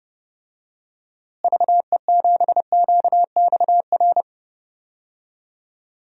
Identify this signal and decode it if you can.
Morse code, heard as a keyed tone, about 32 wpm, 700 Hz. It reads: VE7QXR
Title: DXCC-100 Call Signs - 6 Characters - Repeated 2x - 30wpm